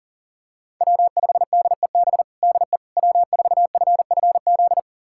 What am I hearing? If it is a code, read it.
Morse code, heard as a keyed tone, about 40 wpm, 700 Hz. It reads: W5DEB DE W4FFZ
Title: USA Callsign DE Callsign - 5 Characters - 40wpm